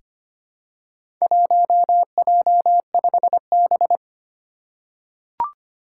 Morse code, heard as a keyed tone, about 25 wpm, 700 Hz. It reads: 1J5B
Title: Four Challenging Letter-Number Combinations 25wpm